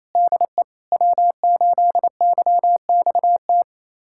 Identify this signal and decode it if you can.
Morse code, heard as a keyed tone, about 28 wpm, 700 Hz. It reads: DE W8YXT
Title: USA Callsign DE Callsign - 5 Characters - 28wpm